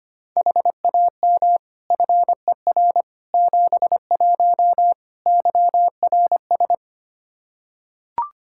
Morse code, heard as a keyed tone, about 25 wpm, 700 Hz. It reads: HAM FER 71 YRS
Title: K1USN SST QSO Practice 25wpm